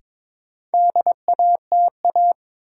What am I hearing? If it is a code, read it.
Morse code, heard as a keyed tone, about 22 wpm, 700 Hz. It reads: DATA
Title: Sentences from Top 1000 Words - Review Part 3 - New Word Drill - 22wpm